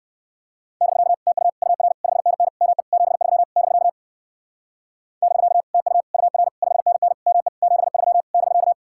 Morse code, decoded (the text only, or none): <BK> TU UR 5NN ME 73 <BK> <BK> TU UR 5NN ME 73 <BK>